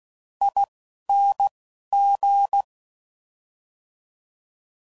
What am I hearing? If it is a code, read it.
Morse code, heard as a keyed tone, about 16 wpm, spaced as for 11 wpm, 800 Hz. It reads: ING
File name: Suffixes / Special INGs / Fast INGs